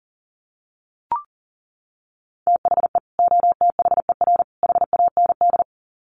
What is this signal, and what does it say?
Morse code, heard as a keyed tone, about 40 wpm, 700 Hz. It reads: THE OTHER HAND
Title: Sets of 3 Words 40wpm